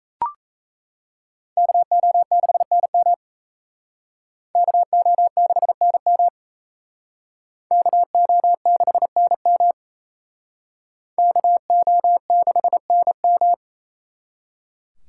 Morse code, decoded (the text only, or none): KO6NM KO6NM KO6NM KO6NM